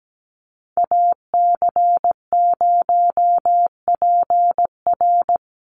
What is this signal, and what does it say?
Morse code, heard as a keyed tone, about 17 wpm, 700 Hz. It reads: AC0PR
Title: NA SOTA Chaser Callsigns 17wpm